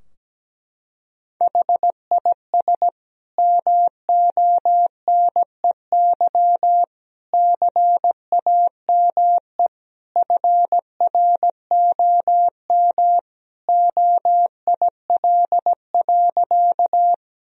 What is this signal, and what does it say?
Morse code, heard as a keyed tone, about 17 wpm, 700 Hz. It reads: HIS MONEY CAME FROM OIL.
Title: Sentences from Top 1000 Words - Review Part 4 - 17wpm